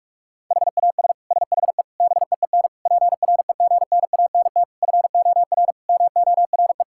Morse code, heard as a keyed tone, about 46 wpm, 700 Hz. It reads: HAS SHE BEEN PREGNANT FOR MORE
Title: Sentences from Top 1000 Words - Review Part 6 - 45wpm